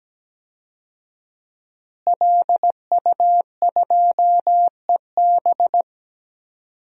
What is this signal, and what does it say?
Morse code, heard as a keyed tone, about 17 wpm, 700 Hz. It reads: LU2EB
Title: DXCC-100 Call Signs - 5 Characters - Repeated 3x - 17wpm